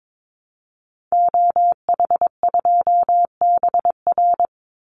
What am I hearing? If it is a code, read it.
Morse code, heard as a keyed tone, about 22 wpm, 700 Hz. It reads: OH2BR
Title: Call Signs Repeated 3x - 22wpm